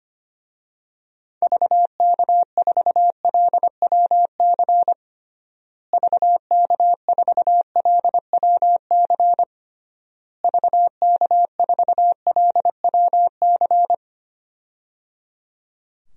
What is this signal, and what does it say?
Morse code, heard as a keyed tone, about 25 wpm, 700 Hz. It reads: VK4LWC VK4LWC VK4LWC
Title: DXCC-100 Call Signs - 6 Characters - Repeated 3x - 25wpm